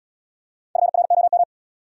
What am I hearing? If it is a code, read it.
Morse code, heard as a keyed tone, about 70 wpm, 700 Hz. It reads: FROM